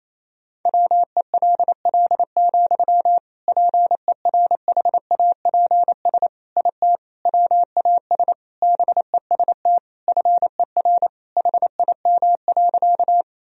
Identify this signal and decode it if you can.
Morse code, heard as a keyed tone, about 28 wpm, 700 Hz. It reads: WELL, PERHAPS IT WAS BEST FER HIM.
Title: Sentences from Top 500 Words and Abbreviations 28wpm